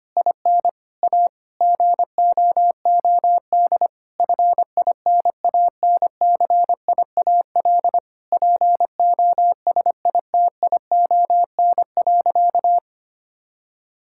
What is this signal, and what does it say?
Morse code, heard as a keyed tone, about 25 wpm, 700 Hz. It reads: IN A GOOD FINANCIAL POSITION.
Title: Sentences from Top 1000 Words - Review Part 5 - 25wpm